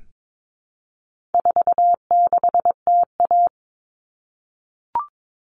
Morse code, heard as a keyed tone, about 22 wpm, 700 Hz. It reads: 46TA
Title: All Cut Numbers - 4-Digit 22wpm